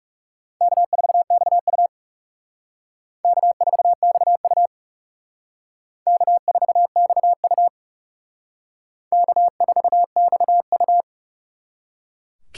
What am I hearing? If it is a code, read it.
Morse code, heard as a keyed tone, about 37 wpm, 700 Hz. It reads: K4XU K4XU K4XU K4XU
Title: Speed Racing - USA 1x2 Call Signs 30wpm